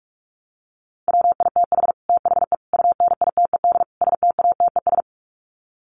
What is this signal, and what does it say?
Morse code, heard as a keyed tone, about 44 wpm, 700 Hz. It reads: WITH THE UNITED STATES
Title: Sets of 4 Words 45wpm